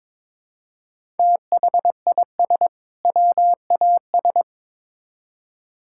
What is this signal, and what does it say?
Morse code, heard as a keyed tone, about 22 wpm, 700 Hz. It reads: THIS WAS
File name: Build A Phrase - 2-Word Phrases - 22wpm